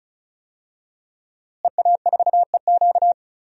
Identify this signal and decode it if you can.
Morse code, heard as a keyed tone, about 34 wpm, 700 Hz. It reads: EA4EQ
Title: DXCC-100 Call Signs - 5 Characters - Repeated 3x - 35wpm